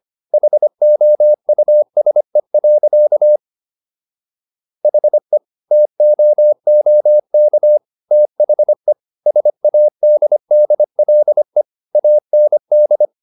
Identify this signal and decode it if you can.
Morse code, heard as a keyed tone, about 25 wpm, 600 Hz. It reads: HOUSE. HE TOOK THE SADDLE AND